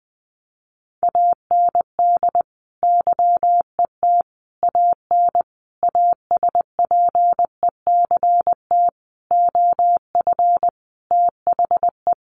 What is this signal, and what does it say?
Morse code, heard as a keyed tone, about 20 wpm, 700 Hz. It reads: AND YET AN ASPECT OF THE
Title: Sentences from Top 1300 Words 20wpm